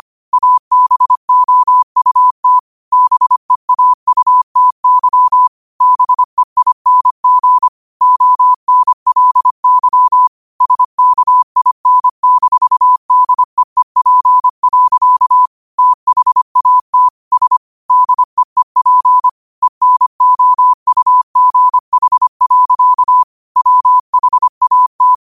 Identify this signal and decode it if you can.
Morse code, heard as a keyed tone, about 25 wpm, 1000 Hz. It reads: ABOUT BEAUTY BEING ONLY SKIN-DEEP. THAT S DEEP ENOUGH. WHAT
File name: 25wpm